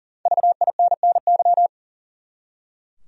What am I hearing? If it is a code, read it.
Morse code, heard as a keyed tone, about 40 wpm, 700 Hz. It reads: VINNY
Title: USA Common Names - Max 5 Letters - 40wpm